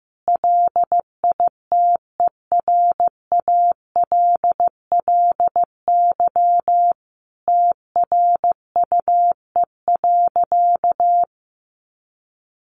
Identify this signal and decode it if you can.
Morse code, heard as a keyed tone, about 15 wpm, 700 Hz. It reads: LITERALLY TRUE.
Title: Sentences from Top 1400 Words 15wpm